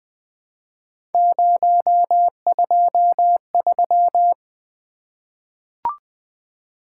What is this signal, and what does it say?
Morse code, heard as a keyed tone, about 20 wpm, 700 Hz. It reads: 023
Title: Three Letter-Number Combinations - 20wpm